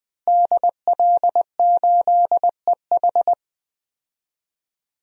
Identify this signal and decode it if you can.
Morse code, heard as a keyed tone, about 20 wpm, 700 Hz. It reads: DL8EH
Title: DXCC-100 Call Signs - 5 Characters - Repeated 2x - 20wpm